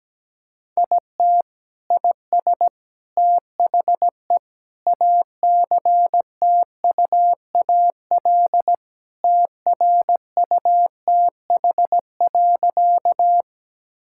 Morse code, heard as a keyed tone, about 17 wpm, 700 Hz. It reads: IT IS THE ACTUAL TRUTH.
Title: Sentences from Top 1000 Words - Review Part 5 - 17wpm